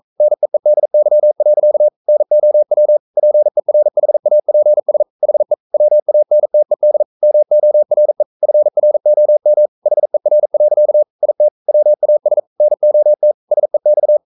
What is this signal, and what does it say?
Morse code, heard as a keyed tone, about 42 wpm, 600 Hz. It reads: NEEDY. NOW PERHAPS HE WANTED MORE FROM HER. IT WAS NOT SEX
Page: Romanen Never af Ken Follet på engelsk.